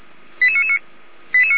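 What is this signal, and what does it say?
Five-tone selective call (ZVEI), heard as quick runs of five tones, tones of about 70 ms. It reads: 88999
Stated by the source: ZVEI five-tone selective call 88999 received by SDR.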